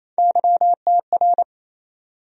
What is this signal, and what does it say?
Morse code, heard as a keyed tone, about 27 wpm, 700 Hz. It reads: YTR
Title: USA Call Sign Postfixes - 3 Letters 28wpm